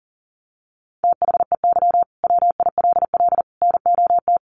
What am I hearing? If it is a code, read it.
Morse code, heard as a keyed tone, about 40 wpm, 700 Hz. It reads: THEY WILL NOT
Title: Sentences from Top 1400 Words 40wpm